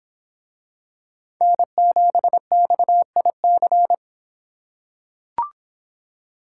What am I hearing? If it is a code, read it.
Morse code, heard as a keyed tone, about 23 wpm, 700 Hz. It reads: N7XIC